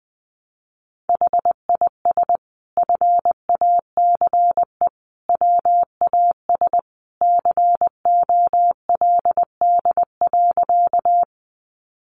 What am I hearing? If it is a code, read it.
Morse code, heard as a keyed tone, about 20 wpm, 700 Hz. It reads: HIS FACE WAS COLD.
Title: A Sound Of Thunder 20wpm